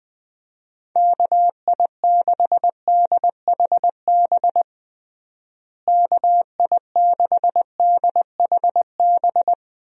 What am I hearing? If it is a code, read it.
Morse code, heard as a keyed tone, about 20 wpm, 700 Hz. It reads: KI6DHB KI6DHB
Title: USA Call Signs - 6 Characters - Repeated 3x - 20wpm